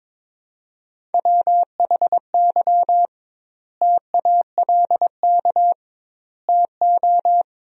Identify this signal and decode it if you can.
Morse code, heard as a keyed tone, about 22 wpm, 700 Hz. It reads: WHY TALK TO
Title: 2x Word-Spacing - Sentences from Top 400 Words - 22wpm